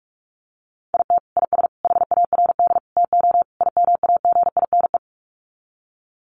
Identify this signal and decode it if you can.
Morse code, heard as a keyed tone, about 44 wpm, 700 Hz. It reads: IT IS HARD TO IMAGINE
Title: Sets of 5 Words Encore2 45wpm